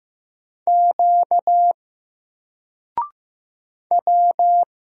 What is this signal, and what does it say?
Morse code, heard as a keyed tone, about 15 wpm, 700 Hz. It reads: Q W
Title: Single Letter-Number 15wpm